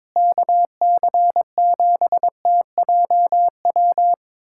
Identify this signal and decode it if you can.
Morse code, heard as a keyed tone, about 22 wpm, 700 Hz. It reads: KC7TJW
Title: USA Callsign DE Callsign - 6 Characters - 22wpm